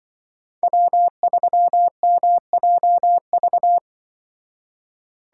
Morse code, heard as a keyed tone, about 25 wpm, 700 Hz. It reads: W3MJV
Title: Speed Racing - USA 1x3 Call Signs 20wpm